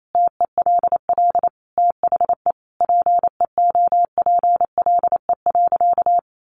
Morse code, heard as a keyed tone, about 28 wpm, 700 Hz. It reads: TELL THE PEOPLE.
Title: Sentences from Top 1200 Words 28wpm